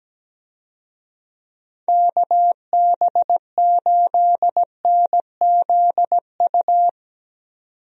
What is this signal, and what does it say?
Morse code, heard as a keyed tone, about 17 wpm, 700 Hz. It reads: KB8NZU
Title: USA Call Signs - 6 Characters - 17wpm